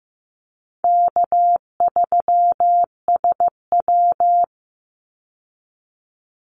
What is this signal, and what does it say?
Morse code, heard as a keyed tone, about 15 wpm, 700 Hz. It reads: K3SW
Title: Easy Call Signs Repeated 3x - 15wpm